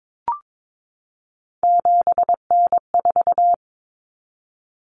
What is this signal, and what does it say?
Morse code, heard as a keyed tone, about 22 wpm, 700 Hz. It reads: 7N4
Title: Common Cut Numbers - 3-Digit 22wpm